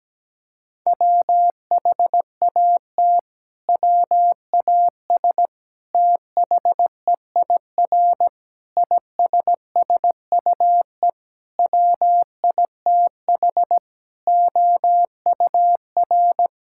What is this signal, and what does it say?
Morse code, heard as a keyed tone, about 17 wpm, 700 Hz. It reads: WHAT WAS THEIR ISSUE WITH OUR
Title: Sentences from Top 1000 Words - Review Part 6 - 17wpm